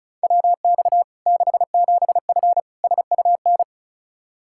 Morse code, heard as a keyed tone, about 36 wpm, 700 Hz. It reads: WX 67F SUN
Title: K1USN SST QSO Practice 35wpm